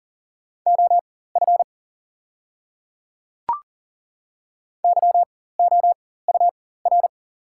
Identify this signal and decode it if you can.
Morse code, heard as a keyed tone, about 40 wpm, spaced as for 19 wpm, 690 Hz. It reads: OF YOUR
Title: 4x Character Spacing - Top 100 Words - 40wpm